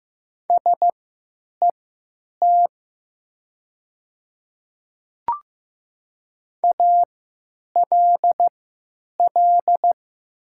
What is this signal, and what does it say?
Morse code, heard as a keyed tone, about 15 wpm, spaced as for 9 wpm, 700 Hz. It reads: SET ALL
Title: 3x Character Spacing - Top 200 Words - 15wpm